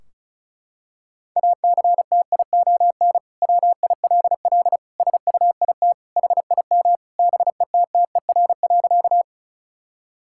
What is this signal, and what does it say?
Morse code, heard as a keyed tone, about 34 wpm, 700 Hz. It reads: ACTION WILL SUIT HIM BETTER.
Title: Sentences from Top 1600 Words 35wpm